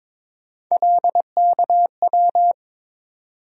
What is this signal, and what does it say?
Morse code, heard as a keyed tone, about 22 wpm, 700 Hz. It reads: LKW
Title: USA Call Sign Postfixes - 3 Letters 22wpm